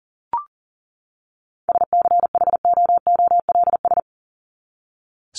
Morse code, heard as a keyed tone, about 40 wpm, 700 Hz. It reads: SCHOOLS
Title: ES:S Suffix - Top 100 Words with Base Words 40wpm